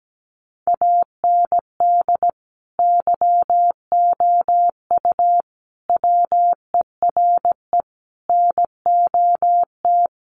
Morse code, heard as a keyed tone, about 17 wpm, 700 Hz. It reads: AND YOU WERE NOT
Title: Sentences from Top 1400 Words 17wpm